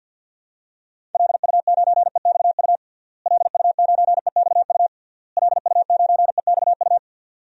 Morse code, heard as a keyed tone, about 51 wpm, 700 Hz. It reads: LU9EXU LU9EXU LU9EXU
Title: DXCC-100 Call Signs - 6 Characters - Repeated 3x - 50wpm